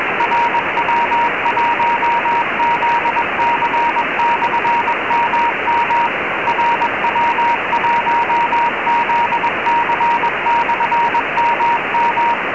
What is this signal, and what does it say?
Morse code, heard as a keyed tone, about 21 wpm, 940 Hz, with radio noise. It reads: RW1ZC/MM RW1ZC/MM
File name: cq cw